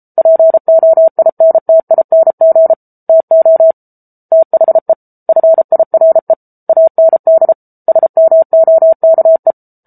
Morse code, heard as a keyed tone, about 33 wpm, 650 Hz. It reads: POINTING TO THE FIRE AND SMOKE
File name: Book Two / 07-Chapter 7 - 33 WPM